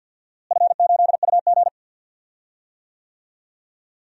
Morse code, heard as a keyed tone, about 52 wpm, 700 Hz. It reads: F8UG
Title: DXCC-100 Call Signs - 4 Characters - Repeated 2x - 50wpm